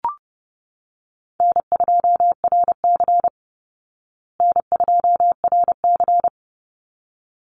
Morse code, heard as a keyed tone, about 30 wpm, 700 Hz. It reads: N2RC N2RC
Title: Easy Call Signs Repeated 2x - 30wpm